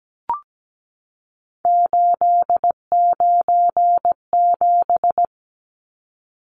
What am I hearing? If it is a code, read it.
Morse code, heard as a keyed tone, about 17 wpm, 700 Hz. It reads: 897
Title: Numbers - 3-Digit 17wpm